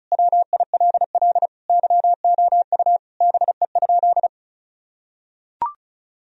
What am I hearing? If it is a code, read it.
Morse code, heard as a keyed tone, about 34 wpm, 700 Hz. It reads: WILL YOU BE?
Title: Sentences from Top 1000 Words - Review Part 4 - 35wpm